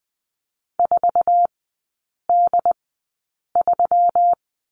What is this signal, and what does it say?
Morse code, heard as a keyed tone, about 20 wpm, spaced as for 8 wpm, 690 Hz. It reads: 4D3
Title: 166 Lesson 27e - Number 7 - Three Characters 20wpm